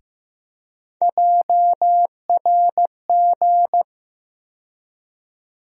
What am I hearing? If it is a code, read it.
Morse code, heard as a keyed tone, about 15 wpm, 700 Hz. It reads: JRG
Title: USA Call Sign Postfixes - 3 Letters 15wpm